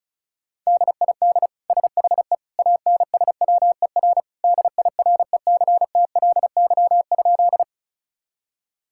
Morse code, heard as a keyed tone, about 36 wpm, 700 Hz. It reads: DID SHE ANSWER DIRECTLY?
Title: Sentences from Top 1000 Words - Review Part 5 - 35wpm